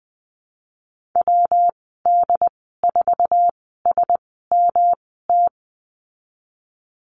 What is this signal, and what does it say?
Morse code, heard as a keyed tone, about 20 wpm, spaced as for 15 wpm, 700 Hz. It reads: WD4SMT
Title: 128 Lesson 21g - Number 6 - Call Signs 20wpm